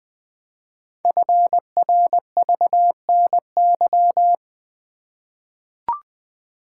Morse code, heard as a keyed tone, about 20 wpm, 700 Hz. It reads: FRVNY
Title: Code Groups - 5 Letters - 20wpm